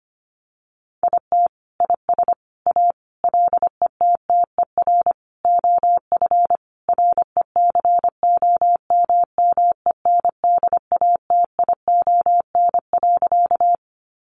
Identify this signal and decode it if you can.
Morse code, heard as a keyed tone, about 25 wpm, 700 Hz. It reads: IT IS A LETTER OF RECOMMENDATION.
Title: Sentences from Top 1400 Words 25wpm